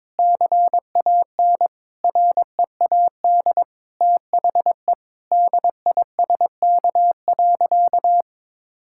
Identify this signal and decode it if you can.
Morse code, heard as a keyed tone, about 22 wpm, 700 Hz. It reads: CAN READ THE DISK.
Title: Sentences from Top 1000 Words - Review Part 6 - 22wpm